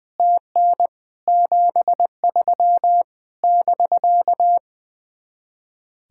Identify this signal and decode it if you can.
Morse code, heard as a keyed tone, about 20 wpm, 700 Hz. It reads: TN 73 <BK>
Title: Speed-Racing Common QSO Phrases Encore - 20wpm